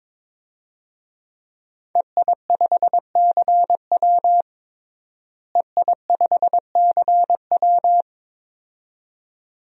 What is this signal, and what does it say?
Morse code, heard as a keyed tone, about 21 wpm, 700 Hz. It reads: EI5CW EI5CW
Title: DXCC-100 Call Signs - 5 Characters - Repeated 2x - 22wpm